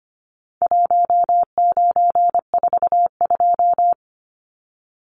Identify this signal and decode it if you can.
Morse code, heard as a keyed tone, about 25 wpm, 700 Hz. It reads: 1942
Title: Numbers - Contest Style - 4-Digit 25wpm